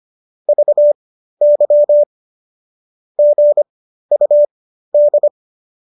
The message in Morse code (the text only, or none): VY GUD